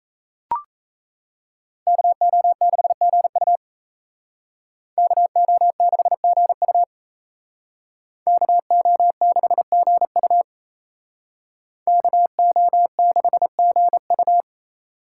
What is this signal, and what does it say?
Morse code, heard as a keyed tone, about 35 wpm, 700 Hz. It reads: KO6GU KO6GU KO6GU KO6GU